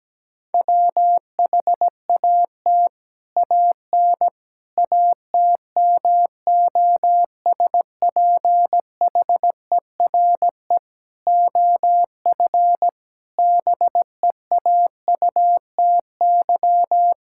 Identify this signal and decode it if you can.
Morse code, heard as a keyed tone, about 17 wpm, 700 Hz. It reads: WHAT AN ATMOSPHERE OF BEAUTY
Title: Sentences from Top 1600 Words 17wpm